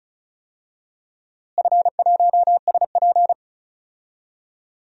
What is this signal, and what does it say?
Morse code, heard as a keyed tone, about 36 wpm, 700 Hz. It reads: F1SP
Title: DXCC-100 Call Signs - 4 Characters - Repeated 2x - 35wpm